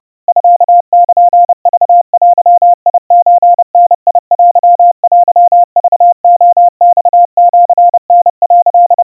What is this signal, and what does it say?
Morse code (Japanese Wabun code), heard as a keyed tone, about 30 wpm, 700 Hz. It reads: ミルクデソダテテクレマシタ」
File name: 110 cpm 700 Hz Crispy